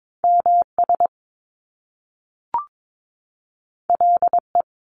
Morse code, heard as a keyed tone, about 22 wpm, 700 Hz. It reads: MS LE
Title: Top Two Letter Combinations in Words 22wpm